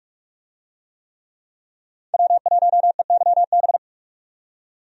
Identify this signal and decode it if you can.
Morse code, heard as a keyed tone, about 42 wpm, 700 Hz. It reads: W1EYB
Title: USA Call Signs - 5 Characters - 45wpm